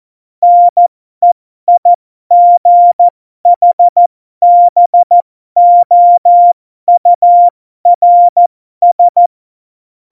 Morse code, heard as a keyed tone, about 14 wpm, spaced as for 12 wpm, 700 Hz. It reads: NEIGHBOURS
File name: Morsekurs / Witze